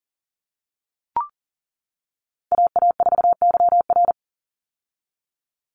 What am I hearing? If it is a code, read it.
Morse code, heard as a keyed tone, about 40 wpm, 700 Hz. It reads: AA4YR
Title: USA Call Signs 40wpm